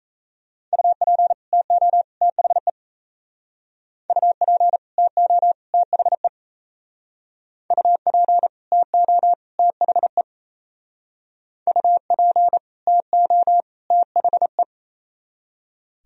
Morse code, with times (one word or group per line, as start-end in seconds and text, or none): 0.7-1.3 UP
1.5-2.0 TO
2.2-2.7 THE
4.1-4.8 UP
5.0-5.5 TO
5.7-6.3 THE
7.7-8.5 UP
8.7-9.3 TO
9.6-10.2 THE
11.7-12.6 UP
12.9-13.6 TO
13.9-14.6 THE